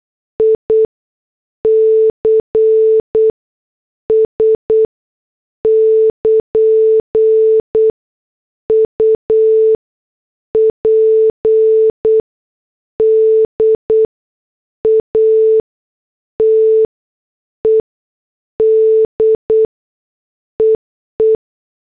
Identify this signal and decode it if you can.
Morse code, heard as a keyed tone, about 8 wpm, spaced as for 6 wpm, 440 Hz. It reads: ICS <KN> UPDATED EE
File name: morse / headlines